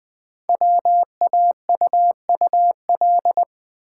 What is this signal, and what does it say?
Morse code, heard as a keyed tone, about 20 wpm, 700 Hz. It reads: WAUUL
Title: Code Groups - 5 Letters - 20wpm